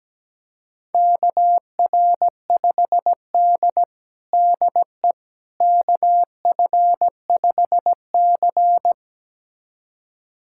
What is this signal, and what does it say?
Morse code, heard as a keyed tone, about 17 wpm, 700 Hz. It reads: KR5D DE KF5C
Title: USA Callsign DE Callsign - 4 Characters - 17wpm